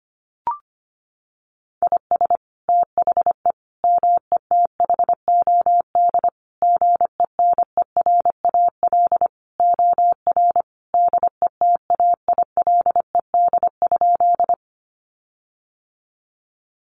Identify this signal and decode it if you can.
Morse code, heard as a keyed tone, about 25 wpm, 700 Hz. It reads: IS THE METHOD GENERAL OR DETAILED?
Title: Sentences from Top 1500 Words 25wpm